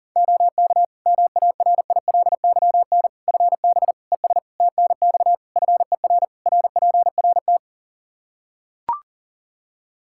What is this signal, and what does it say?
Morse code, heard as a keyed tone, about 40 wpm, 700 Hz. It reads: OK MARILYN FB ES TNX FER RPRT
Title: Practice QSO - Exchange 2 - Info - 40wpm